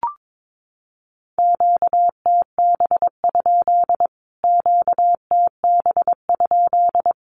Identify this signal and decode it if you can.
Morse code, heard as a keyed tone, about 22 wpm, 700 Hz. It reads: QTB? QTB?
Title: Q-Codes Repeated 3x 22wpm